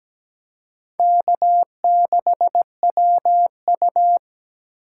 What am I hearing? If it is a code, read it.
Morse code, heard as a keyed tone, about 20 wpm, 700 Hz. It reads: K6WU
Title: Speed Racing - USA 1x2 Call Signs 17wpm